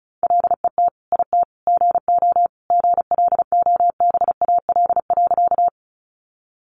Morse code, heard as a keyed tone, about 34 wpm, 700 Hz. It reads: LET IT GO GLOBAL.
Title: Sentences from Top 1200 Words 35wpm